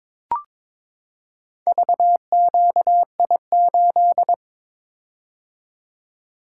Morse code, heard as a keyed tone, about 22 wpm, 700 Hz. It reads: VQI8
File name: practice / 22wpm / Four Letter-Number Combinations - 22wpm